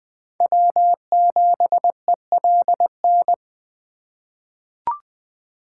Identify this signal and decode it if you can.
Morse code, heard as a keyed tone, about 20 wpm, 700 Hz. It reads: W7ELN
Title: USA Call Signs - 5 Characters - 4x Character Spacing - 20wpm